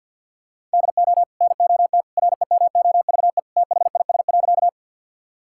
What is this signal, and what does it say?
Morse code, heard as a keyed tone, about 50 wpm, 700 Hz. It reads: DO NOT REMOVE THIS.
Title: Sentences from Top 1000 Words - Review Part 5 - 50wpm